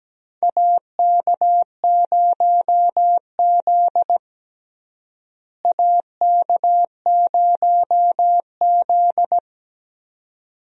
Morse code, heard as a keyed tone, about 17 wpm, 700 Hz. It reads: AK0Z AK0Z
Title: USA Call Signs - 4 Characters - Repeated 2x - 17wpm